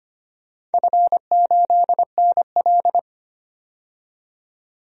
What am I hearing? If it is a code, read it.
Morse code, heard as a keyed tone, about 25 wpm, 700 Hz. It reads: F8NL
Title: DXCC-100 Call Signs - 4 Characters - 25wpm